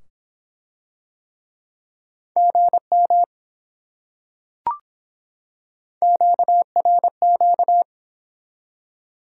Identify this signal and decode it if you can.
Morse code, heard as a keyed tone, about 23 wpm, 700 Hz. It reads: GM QRQ